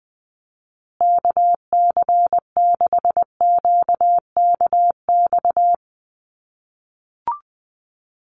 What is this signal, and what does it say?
Morse code, heard as a keyed tone, about 20 wpm, 700 Hz. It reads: KC6QKX
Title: USA Call Signs Repeated 3x 20wpm